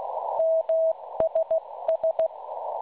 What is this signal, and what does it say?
Morse code, heard as a keyed tone, about 16 wpm, 650 Hz. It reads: MSS
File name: cwronde / websdr recording .6kHz